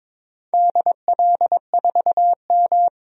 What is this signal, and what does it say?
Morse code, heard as a keyed tone, about 22 wpm, 690 Hz. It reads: DL4M
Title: DXCC-100 Call Signs - 4 Characters - 22wpm